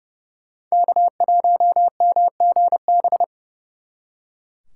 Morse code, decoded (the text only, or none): K1MGB